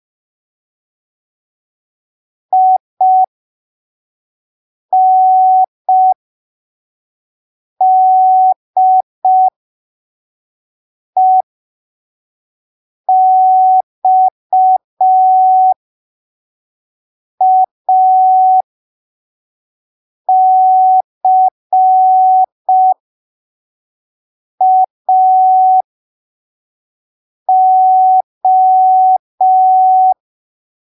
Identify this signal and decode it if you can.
Morse code, heard as a keyed tone, about 5 wpm, spaced as for 3 wpm, 750 Hz. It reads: INDEXACAO
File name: arquivos / texto / texto 09 05ppm